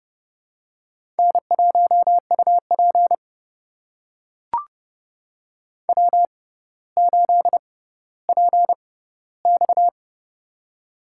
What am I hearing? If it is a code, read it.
Morse code, heard as a keyed tone, about 31 wpm, spaced as for 15 wpm, 700 Hz. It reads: N1UP W8PX
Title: USA Call Signs - 4 Characters - 6x Character Spacing - 30wpm